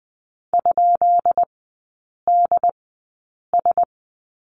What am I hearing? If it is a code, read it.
Morse code, heard as a keyed tone, about 20 wpm, spaced as for 8 wpm, 690 Hz. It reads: ?DS
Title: 133 Lesson 22e - Question Mark - Three Characters 20wpm